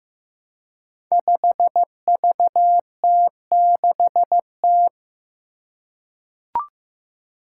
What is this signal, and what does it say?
Morse code, heard as a keyed tone, about 15 wpm, 700 Hz. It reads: 5VT6T
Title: Five Challenging Letter-Number Combinations 15wpm